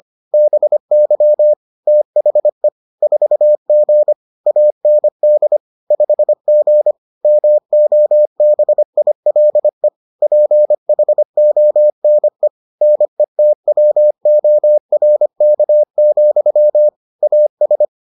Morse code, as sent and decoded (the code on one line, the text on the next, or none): -... -.-- / - .... / ....- --. / .- -. -.. / ..... --. / -- --- -... .. .-.. / .--. .... --- -. / -. - .-- --- .-. -.- --..-- / .- ...
BY THE 4G AND 5G MOBILE PHONE NETWORK, AS